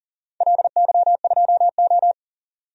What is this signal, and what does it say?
Morse code, heard as a keyed tone, about 40 wpm, 700 Hz. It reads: LY2O
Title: DXCC-100 Call Signs - 4 Characters - 40wpm